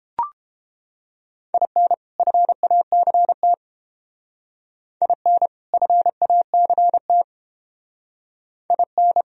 Morse code, heard as a keyed tone, about 28 wpm, 700 Hz. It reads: IN FACT IN FACT IN